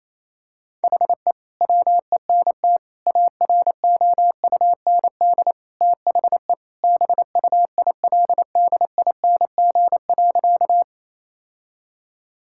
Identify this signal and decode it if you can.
Morse code, heard as a keyed tone, about 28 wpm, 700 Hz. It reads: HE WENT AROUND THE BUILDING.
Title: Sentences from Top 1000 Words - Review Part 5 - 28wpm